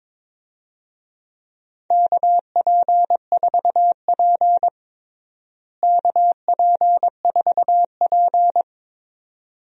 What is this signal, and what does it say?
Morse code, heard as a keyed tone, about 22 wpm, 700 Hz. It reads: KP4P KP4P
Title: DXCC-100 Call Signs - 4 Characters - Repeated 2x - 22wpm